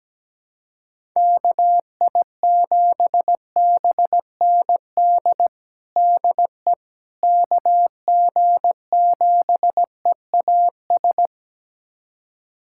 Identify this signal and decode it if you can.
Morse code, heard as a keyed tone, about 17 wpm, 700 Hz. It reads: KI7BND DE KG7EAS
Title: USA Callsign DE Callsign - 6 Characters - 17wpm